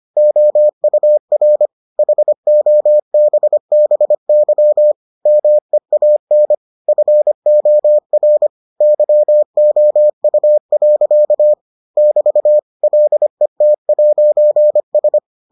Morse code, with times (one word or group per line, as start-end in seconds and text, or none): 0.2-1.7 OUR
2.0-4.9 HOBBY
5.3-6.5 MEAN
6.9-8.5 FOR
8.8-11.5 YOU.
12.0-12.6 <BT>
12.8-15.2 LET'S